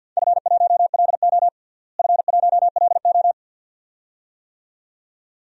Morse code, heard as a keyed tone, about 52 wpm, 700 Hz. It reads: F1LO F1LO
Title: DXCC-100 Call Signs - 4 Characters - Repeated 2x - 50wpm